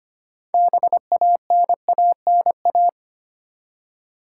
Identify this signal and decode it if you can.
Morse code, heard as a keyed tone, about 25 wpm, 690 Hz. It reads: BANANA
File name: practice / 25wpm / Easy Words with Hints - 25wpm